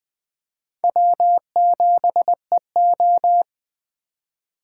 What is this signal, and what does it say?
Morse code, heard as a keyed tone, about 24 wpm, 700 Hz. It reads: W7EO
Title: Speed Racing - USA 1x2 Call Signs 20wpm